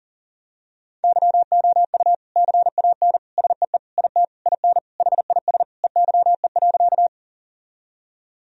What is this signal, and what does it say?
Morse code, heard as a keyed tone, about 37 wpm, 700 Hz. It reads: YOU CAN SEE IT IN HIS EYE.
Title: Sentences from Top 1000 Words - Review Part 1 - 40wpm